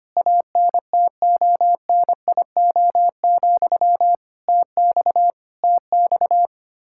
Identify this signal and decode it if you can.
Morse code, heard as a keyed tone, about 25 wpm, 700 Hz. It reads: ANTONIO, TX TX
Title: K1USN SST QSO Practice 25wpm